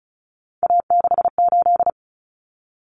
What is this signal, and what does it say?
Morse code, heard as a keyed tone, about 34 wpm, 700 Hz. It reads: A68
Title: Common Cut Numbers - 3-Digit 35wpm